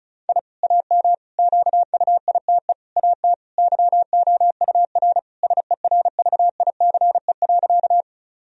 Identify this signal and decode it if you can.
Morse code, heard as a keyed tone, about 34 wpm, 700 Hz. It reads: I AM QUITE AT YOUR SERVICE.
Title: Sentences from Top 1000 Words - Review Part 4 - 35wpm